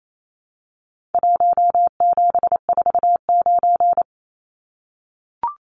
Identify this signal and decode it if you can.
Morse code, heard as a keyed tone, about 28 wpm, 700 Hz. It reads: 1749
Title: Numbers - Contest Style - 4-Digit 28wpm